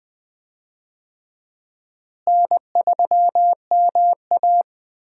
Morse code, heard as a keyed tone, about 20 wpm, 700 Hz. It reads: N3MA
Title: USA Call Signs - 4 Characters - Repeated 3x - 20wpm